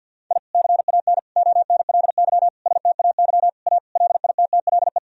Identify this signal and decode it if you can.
Morse code, heard as a keyed tone, about 51 wpm, 700 Hz. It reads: I CAN ONLY STAY A LITTLE
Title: Sentences from Top 1000 Words - Review Part 3 - 50wpm